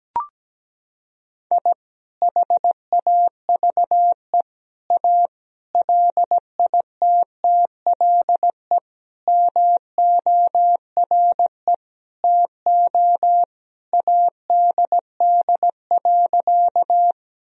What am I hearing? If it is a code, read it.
Morse code, heard as a keyed tone, about 17 wpm, 700 Hz. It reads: I HAVE A LITTLE MORE TO ADD.
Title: Sentences from Top 1000 Words - Review Part 1 - 17wpm